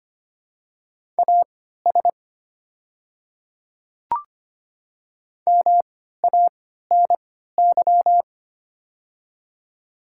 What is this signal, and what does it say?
Morse code, heard as a keyed tone, about 25 wpm, spaced as for 14 wpm, 700 Hz. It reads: AS MANY
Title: 3x Character Spacing - Top 100 Words - 25wpm